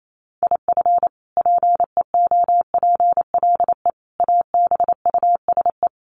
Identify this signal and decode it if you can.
Morse code, heard as a keyed tone, about 28 wpm, 700 Hz. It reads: IF PEOPLE ABUSE
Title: Sentences from Top 1300 Words 28wpm